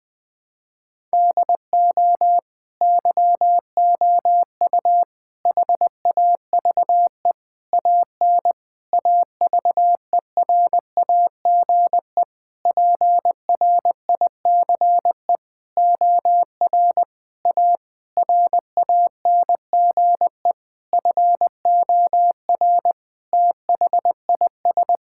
Morse code, heard as a keyed tone, about 20 wpm, 700 Hz. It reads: DO YOU HAVE AN AVERAGE PRICE OR A RANGE FOR THIS
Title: Sentences with 9-Word Phrases - Spoken Beginning - 20wpm